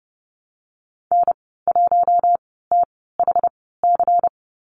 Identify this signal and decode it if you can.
Morse code, heard as a keyed tone, about 30 wpm, spaced as for 17 wpm, 700 Hz. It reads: N1THC
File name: Lesson 19g - Letter w - Call Signs 30wpm